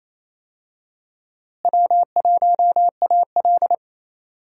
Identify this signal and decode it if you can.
Morse code, heard as a keyed tone, about 28 wpm, 700 Hz. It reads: W1AL
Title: USA Call Signs - 4 Characters - Repeated 2x - 28wpm